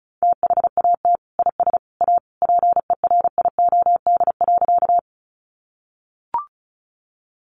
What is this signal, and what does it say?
Morse code, heard as a keyed tone, about 34 wpm, 700 Hz. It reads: THAT IS A PERIOD.
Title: Sentences from Top 800 Words 35wpm